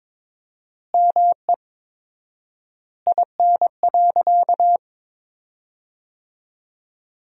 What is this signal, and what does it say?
Morse code, heard as a keyed tone, about 22 wpm, 700 Hz. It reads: ME IN.
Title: 4x Word-Spacing - Sentences from Top 500 Words - 22wpm